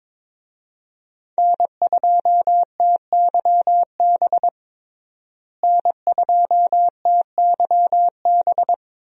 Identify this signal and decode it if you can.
Morse code, heard as a keyed tone, about 22 wpm, 700 Hz. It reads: N2TYB N2TYB
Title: USA Call Signs - 5 Characters - Repeated 3x - 22wpm